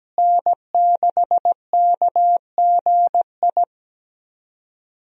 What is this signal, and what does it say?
Morse code, heard as a keyed tone, about 17 wpm, 700 Hz. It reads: N6KGI
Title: USA Call Signs - 5 Characters - Repeated 2x - 17wpm